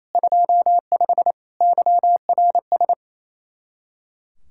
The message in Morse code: 25 YRS

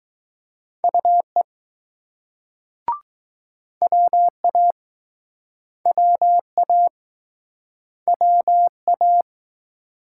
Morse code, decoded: UE WA WA WA

